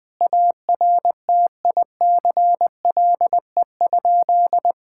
Morse code, heard as a keyed tone, about 20 wpm, 700 Hz. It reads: ARTICLE?
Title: Sentences from Top 1000 Words - Review Part 5 - 20wpm